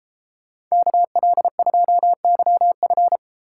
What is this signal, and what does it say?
Morse code, heard as a keyed tone, about 33 wpm, 700 Hz. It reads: KL2YF
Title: Speed Racing - USA 2x2 Call Signs 28wpm